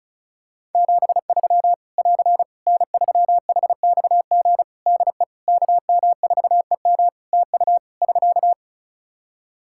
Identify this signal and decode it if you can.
Morse code, heard as a keyed tone, about 34 wpm, 700 Hz. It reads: 73 <AR> N3HXG DE KM4EM TU <SK>
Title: Practice QSO - Exchange 4 - Ending - 35wpm